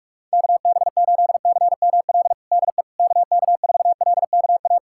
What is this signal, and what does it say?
Morse code, heard as a keyed tone, about 41 wpm, 700 Hz. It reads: KB8CML DE KK4LKA
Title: USA Callsign DE Callsign 45wpm